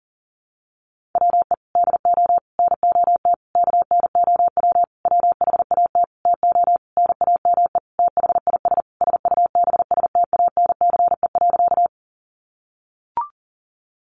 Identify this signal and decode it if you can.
Morse code, heard as a keyed tone, about 40 wpm, 700 Hz. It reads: WE DO NOT KNOW WHAT TO NAME THIS SUBSTANCE.
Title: Sentences from Top 1500 Words 40wpm